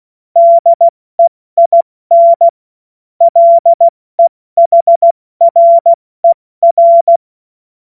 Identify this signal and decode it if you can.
Morse code, heard as a keyed tone, about 16 wpm, 680 Hz. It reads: DEIN LEHRER
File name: Morsekurs / Witze / Witz 14. September 2021, 70 F80